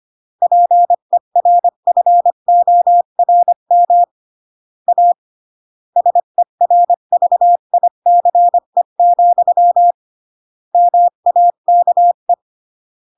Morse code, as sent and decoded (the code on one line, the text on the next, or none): .--. .-. ..-. --- .-. -- / .- / ... .-. ...- .. -.-. --..-- / -- .- -.- .
PERFORM A SERVICE, MAKE